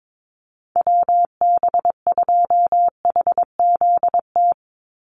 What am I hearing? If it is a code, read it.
Morse code, heard as a keyed tone, about 22 wpm, 700 Hz. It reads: WB2HZT
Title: USA Call Signs 22wpm